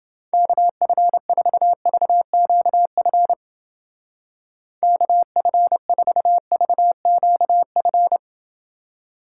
Morse code, heard as a keyed tone, about 30 wpm, 700 Hz. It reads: KF4VQF KF4VQF